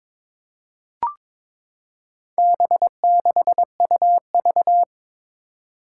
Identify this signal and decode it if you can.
Morse code, heard as a keyed tone, about 21 wpm, 700 Hz. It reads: B6UV
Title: Four Challenging Letter-Number Combinations 22wpm